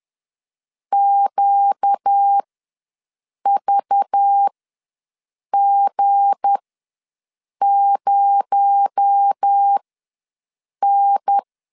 Morse code, heard as a keyed tone, about 10 wpm, spaced as for 6 wpm, 800 Hz. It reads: QVG0N
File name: LEZIONI / 7 GF9Z / 7 G ws 10wpm 3843